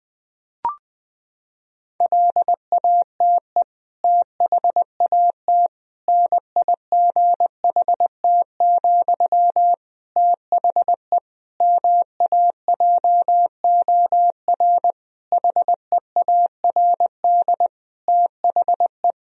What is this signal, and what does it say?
Morse code, heard as a keyed tone, about 20 wpm, 700 Hz. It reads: LATE THAT NIGHT, THE MAJOR HEARD THE
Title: Sentences from Top 1000 Words - Review Part 4 - 20wpm